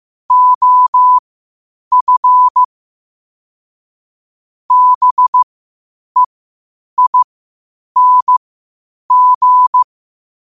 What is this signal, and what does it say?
Morse code, heard as a keyed tone, about 15 wpm, spaced as for 8 wpm, 1000 Hz. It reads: OF BEING
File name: qotdfiles / 8wpm